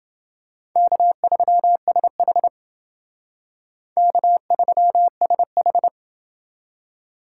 Morse code, heard as a keyed tone, about 28 wpm, 690 Hz. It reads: K3SH K3SH